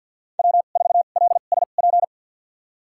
Morse code, heard as a keyed tone, about 51 wpm, spaced as for 36 wpm, 700 Hz. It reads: W4LSP
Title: USA Call Signs - 2x Character Spacing - 50wpm